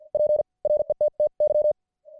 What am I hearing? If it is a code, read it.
Morse code, heard as a keyed tone, about 50 wpm, spaced as for 38 wpm, 600 Hz. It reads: PRETTY